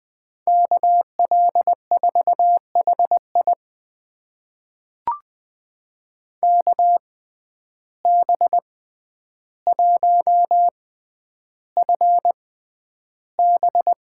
Morse code, heard as a keyed tone, about 20 wpm, spaced as for 10 wpm, 700 Hz. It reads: KL4HI KB1FB
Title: USA Call Signs - 5 Characters - 6x Character Spacing - 20wpm